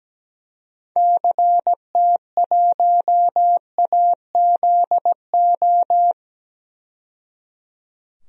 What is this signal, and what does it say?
Morse code, heard as a keyed tone, about 17 wpm, 700 Hz. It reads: CT1AZO
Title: DXCC-100 Call Signs - 6 Characters - 17wpm